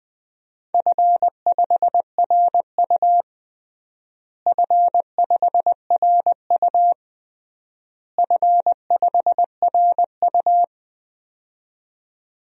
Morse code, heard as a keyed tone, about 20 wpm, 700 Hz. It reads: F5RU F5RU F5RU
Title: DXCC-100 Call Signs - 4 Characters - Repeated 3x - 20wpm